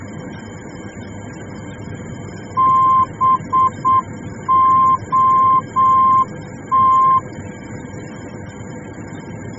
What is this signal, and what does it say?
Morse code, heard as a keyed tone, about 8 wpm, 1050 Hz. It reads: BOT
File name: germany / mx / BOT-D